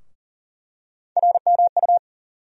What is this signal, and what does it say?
Morse code, heard as a keyed tone, about 40 wpm, 700 Hz. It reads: RMU
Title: USA Call Sign Postfixes - 3 Letters 40wpm